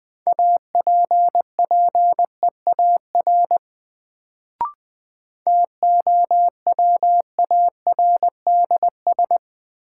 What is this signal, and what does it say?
Morse code, heard as a keyed tone, about 20 wpm, 700 Hz. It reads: APPEAR TOWARDS
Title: Sentences from Top 1000 Words - Review Part 2 - New Word Drill - 20wpm